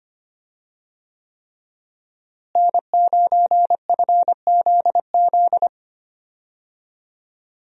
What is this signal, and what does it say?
Morse code, heard as a keyed tone, about 25 wpm, 700 Hz. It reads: N9FZZ
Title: USA Call Signs - 5 Characters - Repeated 3x - 25wpm